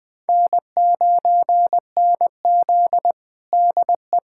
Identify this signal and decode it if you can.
Morse code, heard as a keyed tone, about 20 wpm, 700 Hz. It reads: N9NZ DE
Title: USA Callsign DE Callsign - 4 Characters - 20wpm